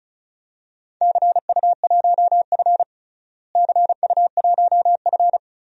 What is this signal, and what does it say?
Morse code, heard as a keyed tone, about 36 wpm, 700 Hz. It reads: CU1F CU1F
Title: DXCC-100 Call Signs - 4 Characters - Repeated 2x - 35wpm